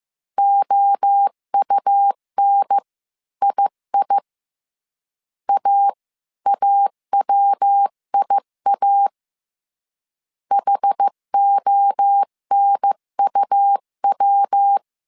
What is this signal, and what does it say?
Morse code, heard as a keyed tone, about 15 wpm, 800 Hz. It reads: OUNII AAWIA HONUW
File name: CW / DET / 1 3 1500C S C 14wpm 2458